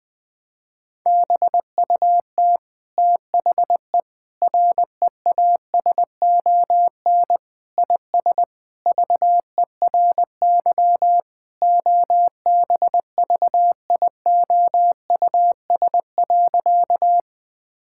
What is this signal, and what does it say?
Morse code, heard as a keyed tone, about 20 wpm, 700 Hz. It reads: BUT THE REASON IS VERY OBVIOUS.
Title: Sentences from Top 1600 Words 20wpm